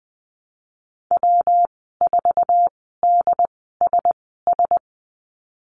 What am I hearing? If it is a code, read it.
Morse code, heard as a keyed tone, about 20 wpm, spaced as for 14 wpm, 700 Hz. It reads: W4DSS
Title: 154 Lesson 25g - Letter p - Call Signs 20wpm